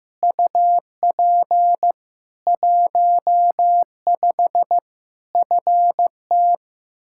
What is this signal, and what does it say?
Morse code, heard as a keyed tone, about 15 wpm, 700 Hz. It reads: UP 15 FT
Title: Practice QSO - Exchange 2 - Info - 15wpm